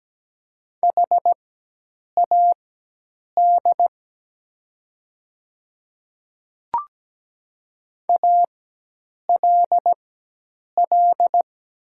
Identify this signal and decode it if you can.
Morse code, heard as a keyed tone, about 17 wpm, spaced as for 8 wpm, 700 Hz. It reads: HAD ALL
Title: 4x Character Spacing - Top 100 Words - 17wpm